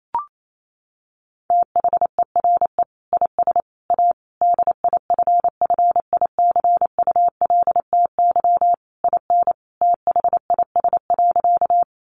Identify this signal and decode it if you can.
Morse code, heard as a keyed tone, about 28 wpm, 700 Hz. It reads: THERE IS A DIFFICULTY IN THIS.
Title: Sentences from Top 1500 Words 28wpm